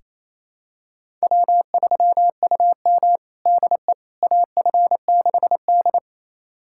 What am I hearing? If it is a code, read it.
Morse code, heard as a keyed tone, about 28 wpm, 690 Hz. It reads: W3UM DE AF6D
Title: USA Callsign DE Callsign - 4 Characters - 28wpm